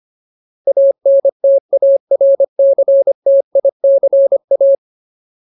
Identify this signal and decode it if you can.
Morse code, heard as a keyed tone, about 25 wpm, 550 Hz. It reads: ANTARCTICA
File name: Ultimate Word List / 25WPM Ultimate